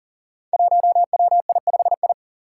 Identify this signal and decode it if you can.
Morse code, heard as a keyed tone, about 40 wpm, 700 Hz. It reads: 1WI5I
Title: Five Challenging Letter-Number Combinations 40wpm